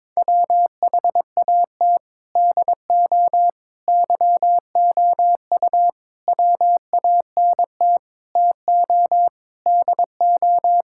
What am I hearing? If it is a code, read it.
Morse code, heard as a keyed tone, about 21 wpm, 700 Hz. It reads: WHAT DO YOU WANT TO DO
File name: practice / 22wpm / Sentences with 4-Word Phrases - 22wpm